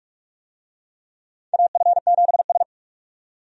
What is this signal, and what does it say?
Morse code, heard as a keyed tone, about 45 wpm, 700 Hz. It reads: AF7S
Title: USA Call Signs - 4 Characters - 45wpm